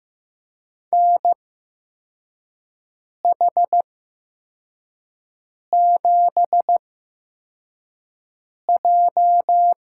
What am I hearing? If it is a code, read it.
Morse code, heard as a keyed tone, about 15 wpm, spaced as for 4 wpm, 700 Hz. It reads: NH7J
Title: USA Call Signs - 5 Characters - 8x Character Spacing - 15wpm